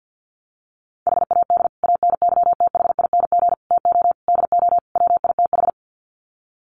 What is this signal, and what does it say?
Morse code, heard as a keyed tone, about 50 wpm, 700 Hz. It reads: HAD ANYTHING TO DO WITH
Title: Sets of 5 Words Encore 50wpm